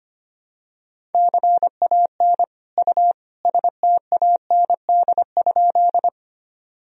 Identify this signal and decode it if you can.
Morse code, heard as a keyed tone, about 25 wpm, 700 Hz. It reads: CAN U STAND?
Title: Sentences from Top 300 Words and Abbreviations 25wpm